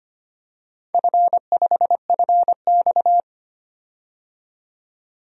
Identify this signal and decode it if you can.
Morse code, heard as a keyed tone, about 25 wpm, 700 Hz. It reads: F5FX
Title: DXCC-100 Call Signs - 4 Characters - 25wpm